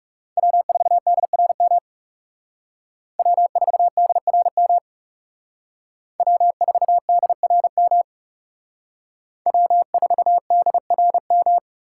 Morse code, heard as a keyed tone, about 38 wpm, 700 Hz. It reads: W4DRM W4DRM W4DRM W4DRM